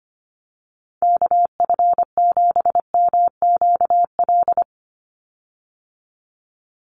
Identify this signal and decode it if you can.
Morse code, heard as a keyed tone, about 25 wpm, 700 Hz. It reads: KF7MQL
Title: USA Call Signs 25wpm